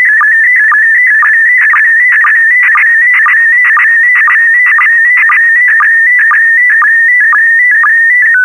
SSTV (slow-scan television), PD 120.
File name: sstv / space comms pd120 sstv test recording